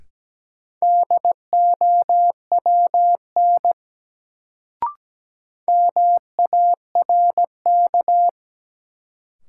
Morse code, heard as a keyed tone, about 17 wpm, 700 Hz. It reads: DOWN MARK
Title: Sentences from Top 300 Words - New Word Drill - 17wpm